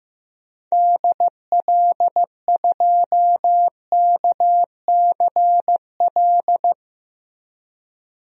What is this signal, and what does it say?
Morse code, heard as a keyed tone, about 15 wpm, 700 Hz. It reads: DL2KCL
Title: DXCC-100 Call Signs - 6 Characters - Repeated 3x - 15wpm